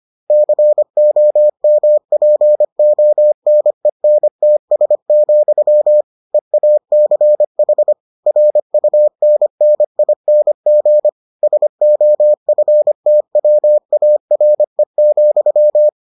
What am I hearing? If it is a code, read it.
Morse code, heard as a keyed tone, about 25 wpm, 600 Hz. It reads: COMPONENTS, EACH RUNNING SOFTWARE,